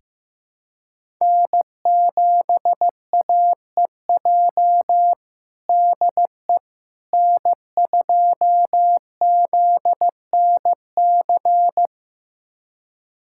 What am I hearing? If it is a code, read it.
Morse code, heard as a keyed tone, about 15 wpm, 700 Hz. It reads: N7AEJ DE N2ZNC
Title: USA Callsign DE Callsign 15wpm